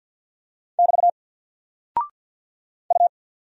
Morse code, heard as a keyed tone, about 48 wpm, 700 Hz. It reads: <BT> U
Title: Single Character (Letter:Number:Punctuation:Prosign) Alt 50wpm